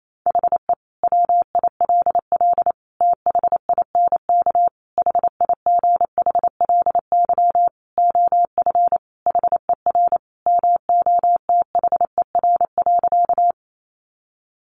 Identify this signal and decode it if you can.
Morse code, heard as a keyed tone, about 28 wpm, 700 Hz. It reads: HE WILL THINK HIGHLY OF HER MOTHER.
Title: Sentences from Top 1100 Words 28wpm